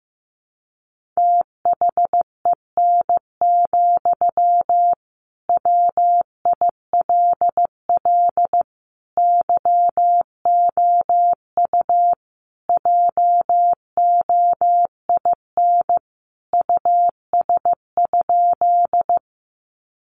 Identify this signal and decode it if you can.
Morse code, heard as a keyed tone, about 15 wpm, 700 Hz. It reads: THEN, WILL YOU JOIN US?
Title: Sentences from Top 1400 Words 15wpm